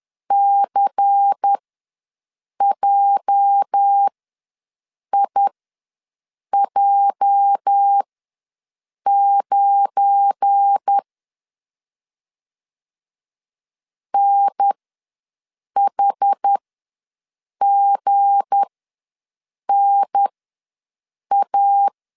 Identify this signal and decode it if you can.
Morse code, heard as a keyed tone, about 10 wpm, spaced as for 6 wpm, 800 Hz. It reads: CJIJ9 NHGNA
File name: LEZIONI / 8 YJ45 / 8 YJ ws 10wpm 4043